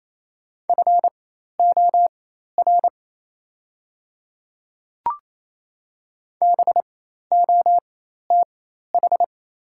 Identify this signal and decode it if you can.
Morse code, heard as a keyed tone, about 28 wpm, spaced as for 13 wpm, 700 Hz. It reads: FOR BOTH
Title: 4x Character Spacing - Top 300 Words - 28wpm